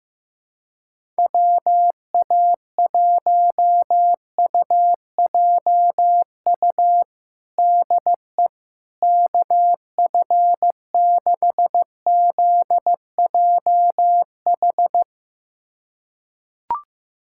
Morse code, heard as a keyed tone, about 15 wpm, 700 Hz. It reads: WA1UJU DE KF6ZJH
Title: USA Callsign DE Callsign 15wpm